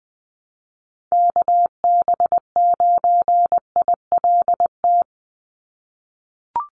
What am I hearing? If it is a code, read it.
Morse code, heard as a keyed tone, about 19 wpm, 700 Hz. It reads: KB9ILT
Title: NA SOTA Chaser Callsigns 20wpm